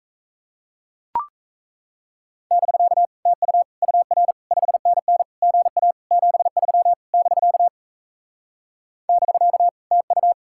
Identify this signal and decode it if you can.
Morse code, heard as a keyed tone, about 38 wpm, 700 Hz. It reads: <BK> TU UR 5NN GA 73 <BK> <BK> TU